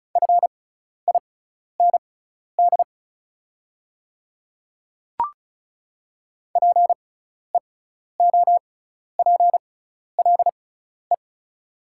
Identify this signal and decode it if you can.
Morse code, heard as a keyed tone, about 36 wpm, spaced as for 12 wpm, 690 Hz. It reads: FIND PEOPLE
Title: 6x Character Spacing - Top 100 Words - 35wpm